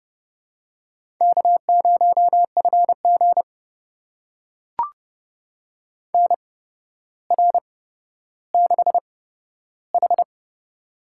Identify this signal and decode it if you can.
Morse code, heard as a keyed tone, about 30 wpm, spaced as for 13 wpm, 700 Hz. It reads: K0FG NR6H
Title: USA Call Signs - 4 Characters - 8x Character Spacing - 30wpm